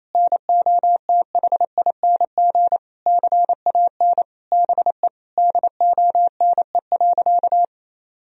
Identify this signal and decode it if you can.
Morse code, heard as a keyed tone, about 28 wpm, 700 Hz. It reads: NOTHING CAN BE DONE.
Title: Sentences from Top 1000 Words - Review Part 1 - 28wpm